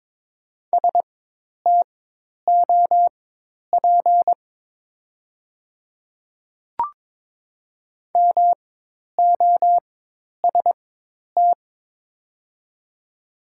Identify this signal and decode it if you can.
Morse code, heard as a keyed tone, about 22 wpm, spaced as for 10 wpm, 690 Hz. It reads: STOP MOST
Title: 4x Character Spacing - Top 500 Words - 22wpm